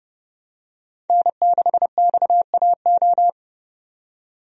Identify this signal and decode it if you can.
Morse code, heard as a keyed tone, about 30 wpm, 700 Hz. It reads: N6XAO
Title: USA Call Signs - 5 Characters - 30wpm